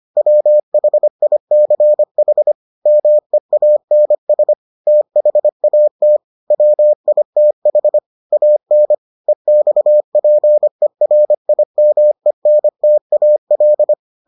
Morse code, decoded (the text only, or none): WHICH MEANS THAT WITH AN EXPERIMENTAL